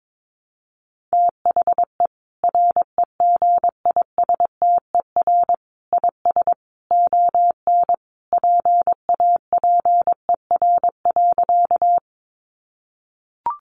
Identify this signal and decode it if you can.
Morse code, heard as a keyed tone, about 22 wpm, 700 Hz. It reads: THE REGISTER IS ON PAPER.
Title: Sentences from Top 1200 Words 22wpm